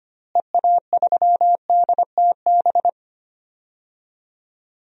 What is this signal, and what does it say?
Morse code, heard as a keyed tone, about 25 wpm, 700 Hz. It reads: EA3DTB
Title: DXCC-100 Call Signs - 6 Characters - Repeated 2x - 25wpm